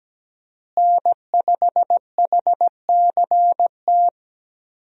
Morse code, heard as a keyed tone, about 17 wpm, 700 Hz. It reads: N5HCT
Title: USA Call Signs - 5 Characters - Repeated 3x - 17wpm